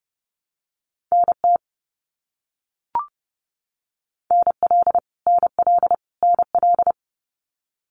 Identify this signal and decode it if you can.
Morse code, heard as a keyed tone, about 30 wpm, 700 Hz. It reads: NT NL NL NL
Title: Canadian Provinces and Territories - Repeated 3x - 30wpm